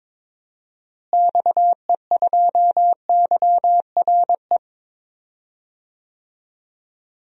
Morse code, heard as a keyed tone, about 22 wpm, 700 Hz. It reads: XE2YRE
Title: DXCC-100 Call Signs - 6 Characters - Repeated 2x - 22wpm